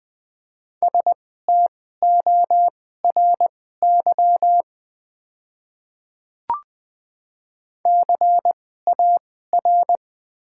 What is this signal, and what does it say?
Morse code, heard as a keyed tone, about 20 wpm, spaced as for 14 wpm, 700 Hz. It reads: STORY CAR
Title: 2x Character Spacing - Top 300 Words - 20wpm